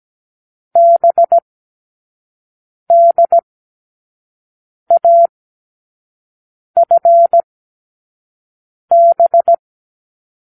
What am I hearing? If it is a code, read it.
Morse code, heard as a keyed tone, about 17 wpm, spaced as for 5 wpm, 680 Hz. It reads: BDAFB
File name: k7qo / Lesson9(A-F)